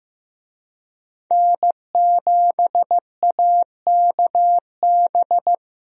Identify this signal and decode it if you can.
Morse code, heard as a keyed tone, about 15 wpm, 700 Hz. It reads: N7AKB
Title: USA Call Signs - 5 Characters - Repeated 2x - 15wpm